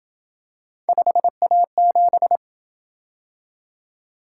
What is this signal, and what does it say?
Morse code, heard as a keyed tone, about 27 wpm, 700 Hz. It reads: 5A7